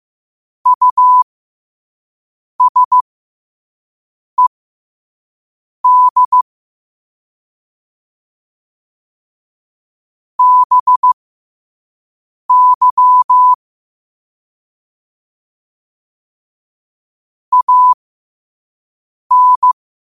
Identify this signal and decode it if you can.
Morse code, heard as a keyed tone, about 15 wpm, spaced as for 5 wpm, 1000 Hz. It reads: USED BY AN
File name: 5wpm